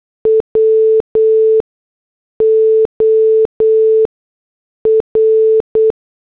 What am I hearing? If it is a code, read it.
Morse code, heard as a keyed tone, about 8 wpm, 440 Hz. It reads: W O R